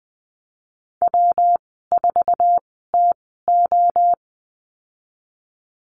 Morse code, heard as a keyed tone, about 19 wpm, spaced as for 14 wpm, 700 Hz. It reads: W4TO
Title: 173 Lesson 28g - Number 9 - Call Signs 20wpm